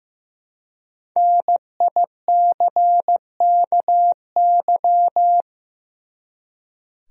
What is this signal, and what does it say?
Morse code, heard as a keyed tone, about 15 wpm, 700 Hz. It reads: NICKY
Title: USA Common Names - 15wpm